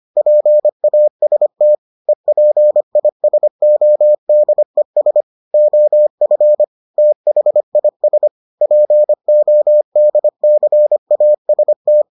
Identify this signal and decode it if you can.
Morse code, heard as a keyed tone, about 25 wpm, 600 Hz. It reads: PAST EPISODES OF THIS PODCAST